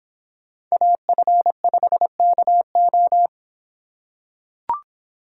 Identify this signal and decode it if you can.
Morse code, heard as a keyed tone, about 25 wpm, 700 Hz. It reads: AF5KO